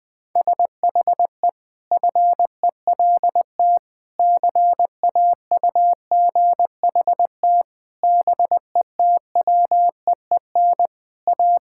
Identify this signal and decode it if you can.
Morse code, heard as a keyed tone, about 21 wpm, 700 Hz. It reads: SHE FELT CAUGHT BETWEEN A
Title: Sentences with 8-Word Phrases - Spoken Beginning - 20wpm